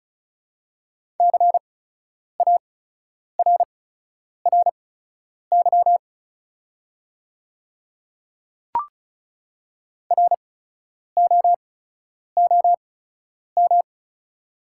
Morse code, heard as a keyed tone, about 36 wpm, spaced as for 10 wpm, 690 Hz. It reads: CARRY ROOM
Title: 8x Character Spacing - Top 300 Words - 35wpm